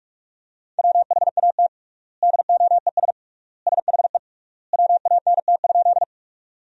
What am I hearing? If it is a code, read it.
Morse code, heard as a keyed tone, about 46 wpm, 700 Hz. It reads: WHAT DOES SHE WANT?
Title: 3x Word-Spacing - Sentences from Top 200 Words - 45wpm